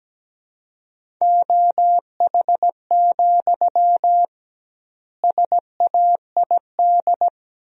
Morse code, heard as a keyed tone, about 17 wpm, 700 Hz. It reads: OH, SAID
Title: 2x Word-Spacing - Sentences from Top 500 Words - 17wpm